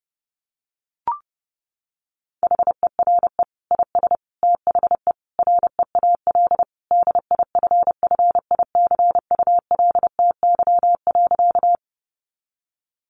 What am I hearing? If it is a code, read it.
Morse code, heard as a keyed tone, about 30 wpm, 700 Hz. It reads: HERE IS THE REAL DIFFICULTY.
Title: Sentences from Top 1500 Words 30wpm